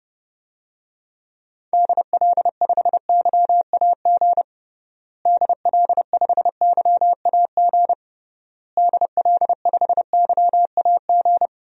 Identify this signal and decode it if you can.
Morse code, heard as a keyed tone, about 32 wpm, 700 Hz. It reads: DL5YAG DL5YAG DL5YAG
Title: DXCC-100 Call Signs - 6 Characters - Repeated 3x - 30wpm